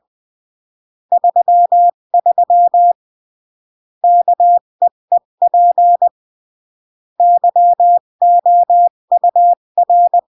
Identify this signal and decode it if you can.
Morse code, heard as a keyed tone, about 20 wpm, 700 Hz. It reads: 33 KEEP YOUR
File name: CW / jodorowsky / 82 commandments 20-15